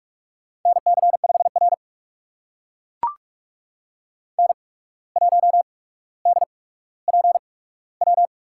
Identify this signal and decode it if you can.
Morse code, heard as a keyed tone, about 42 wpm, spaced as for 17 wpm, 700 Hz. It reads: NC5R N1DPW
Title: USA Call Signs - 8x Character Spacing - 45wpm